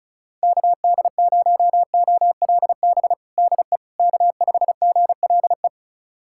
Morse code, heard as a keyed tone, about 34 wpm, 700 Hz. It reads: KD0OLB DE K5GLE
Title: USA Callsign DE Callsign 35wpm